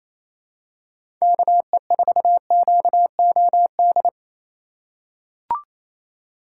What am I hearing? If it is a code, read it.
Morse code, heard as a keyed tone, about 28 wpm, 700 Hz. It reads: KE4QOD
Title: USA Call Signs - 6 Characters - 3x Character Spacing - 28wpm